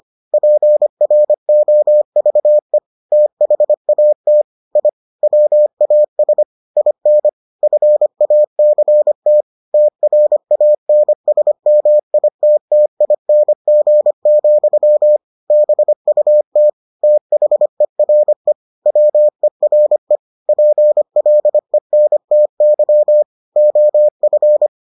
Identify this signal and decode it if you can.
Morse code, heard as a keyed tone, about 25 wpm, 600 Hz. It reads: PROVE THAT I WAS IN FACT TRANSMITTING, BUT THERE WERE PLENTY OF